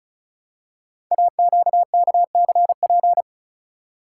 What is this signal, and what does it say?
Morse code, heard as a keyed tone, about 36 wpm, 700 Hz. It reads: AQKCP
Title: Code Groups - 5 Letters - 35wpm